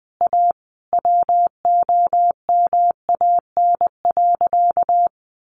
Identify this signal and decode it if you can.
Morse code, heard as a keyed tone, about 20 wpm, 700 Hz. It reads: A WOMAN.
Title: Sentences from Top 1100 Words 20wpm